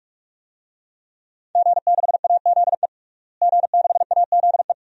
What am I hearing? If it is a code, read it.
Morse code, heard as a keyed tone, about 47 wpm, 700 Hz. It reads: G6AZE G6AZE
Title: DXCC-100 Call Signs - 5 Characters - Repeated 2x - 45wpm